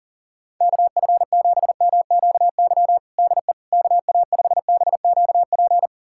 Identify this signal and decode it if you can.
Morse code, heard as a keyed tone, about 40 wpm, 700 Hz. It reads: KF7MQY DE KA5BQP
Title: USA Callsign DE Callsign - 6 Characters - 40wpm